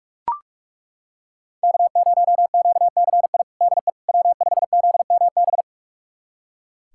Morse code, heard as a keyed tone, about 41 wpm, 700 Hz. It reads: K0QCI DE W5ZMB
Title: USA Callsign DE Callsign - 5 Characters - 45wpm